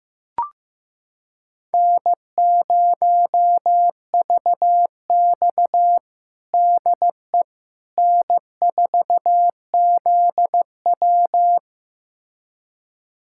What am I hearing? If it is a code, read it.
Morse code, heard as a keyed tone, about 16 wpm, 700 Hz. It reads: N0VX DE N4ZW
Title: USA Callsign DE Callsign - 4 Characters - 15wpm